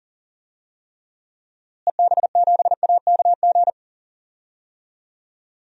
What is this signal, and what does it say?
Morse code, heard as a keyed tone, about 40 wpm, 700 Hz. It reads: EB7AKG
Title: DXCC-100 Call Signs - 6 Characters - Repeated 3x - 40wpm